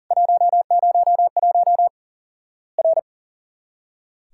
Morse code, heard as a keyed tone, about 40 wpm, 700 Hz. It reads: 101 R
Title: Worked All Europe DX Contest - QTC Exchange - 40wpm